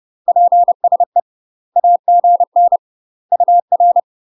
Morse code, heard as a keyed tone, about 30 wpm, 700 Hz. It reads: PSE AGN UR
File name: UK MASTER QSOs TRAINING / 30WPM / UK MASTER QSOs 0039